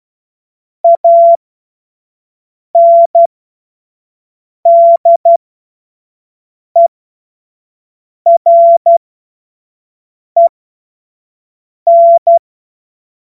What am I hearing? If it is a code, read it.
Morse code, heard as a keyed tone, about 12 wpm, spaced as for 5 wpm, 680 Hz. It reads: ANDEREN
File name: Morsekurs / Witze / Witz 07. Juni 2022 25 F70